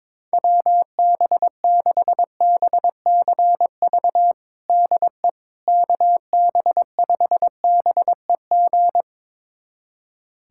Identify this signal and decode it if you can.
Morse code, heard as a keyed tone, about 22 wpm, 700 Hz. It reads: WB6BCV DE KB5BEG
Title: USA Callsign DE Callsign - 6 Characters - 22wpm